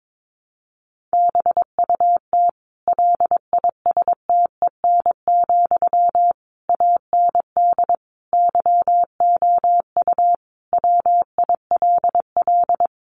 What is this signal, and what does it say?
Morse code, heard as a keyed tone, about 22 wpm, 700 Hz. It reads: BUT LISTEN, AND YOU WILL
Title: Sentences from Top 600 Words 22wpm